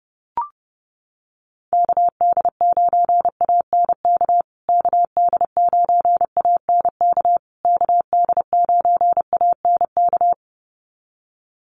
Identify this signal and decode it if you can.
Morse code, heard as a keyed tone, about 30 wpm, 700 Hz. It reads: KD9ANK KD9ANK KD9ANK
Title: USA Call Signs Repeated 3x 30wpm